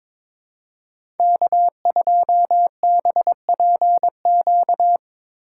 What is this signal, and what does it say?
Morse code, heard as a keyed tone, about 23 wpm, 700 Hz. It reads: K2BPQ
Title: USA Call Signs - 5 Characters - 8x Character Spacing - 22wpm